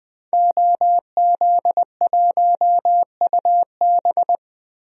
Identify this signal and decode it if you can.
Morse code, heard as a keyed tone, about 20 wpm, 700 Hz. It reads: OZ1UB
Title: DXCC-100 Call Signs - 5 Characters - Repeated 2x - 20wpm